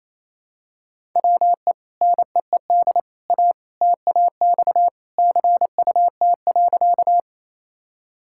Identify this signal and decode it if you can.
Morse code, heard as a keyed tone, about 27 wpm, 700 Hz. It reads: WE NEED A TAX CUT.
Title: Sentences from Top 1000 Words - Review Part 5 - 28wpm